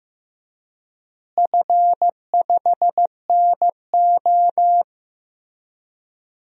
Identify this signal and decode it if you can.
Morse code, heard as a keyed tone, about 15 wpm, 700 Hz. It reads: F5NO
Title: DXCC-100 Call Signs - 4 Characters - Repeated 2x - 15wpm